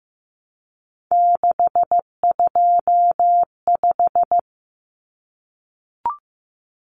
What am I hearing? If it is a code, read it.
Morse code, heard as a keyed tone, about 15 wpm, 700 Hz. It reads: 625
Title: Common Cut Numbers - 3-Digit 15wpm